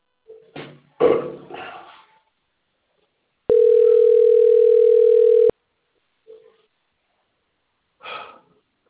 Ringback tone (North America), 440 + 480 Hz.